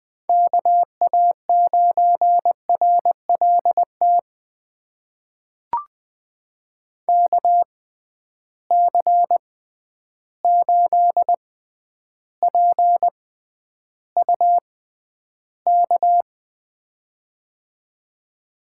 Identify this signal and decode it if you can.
Morse code, heard as a keyed tone, about 20 wpm, spaced as for 10 wpm, 700 Hz. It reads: KA9RLT KC8PUK
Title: USA Call Signs - 6 Characters - 6x Character Spacing - 20wpm